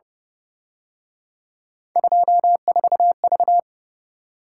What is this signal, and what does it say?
Morse code, heard as a keyed tone, about 31 wpm, 700 Hz. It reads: 24V